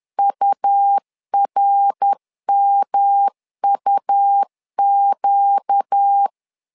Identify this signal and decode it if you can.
Morse code, heard as a keyed tone, about 10 wpm, 800 Hz. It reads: URMUQ
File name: LEZIONI / 4 Q 10wpm 2210